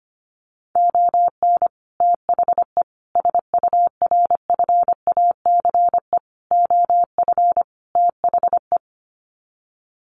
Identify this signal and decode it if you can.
Morse code, heard as a keyed tone, about 25 wpm, 700 Hz. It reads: ON THE SURFACE OF THE
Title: Sets of 5 Words Encore 25wpm